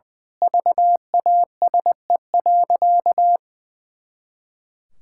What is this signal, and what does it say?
Morse code, heard as a keyed tone, about 20 wpm, 700 Hz. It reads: VASE.
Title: Pangram Challenge v2 - 20wpm